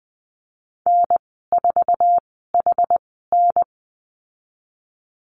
Morse code, heard as a keyed tone, about 20 wpm, spaced as for 15 wpm, 700 Hz. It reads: N4HN
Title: 062 Lesson 11g - Letter h - Call Signs 20wpm